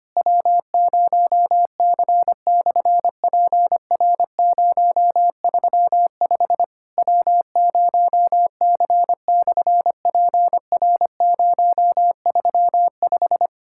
Morse code, heard as a keyed tone, about 25 wpm, 700 Hz. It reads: W0C/PR035 W0C/PR035
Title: SOTA QSO Practice 25wpm